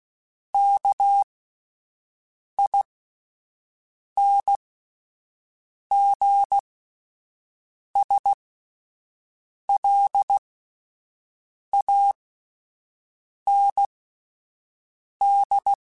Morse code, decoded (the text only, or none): KINGSLAND